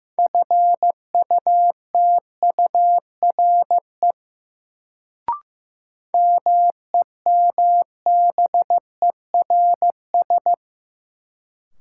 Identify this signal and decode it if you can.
Morse code, heard as a keyed tone, about 15 wpm, 700 Hz. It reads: FUTURE MEMBERS
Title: Sets of 3 Words - New Word Drill - 15wpm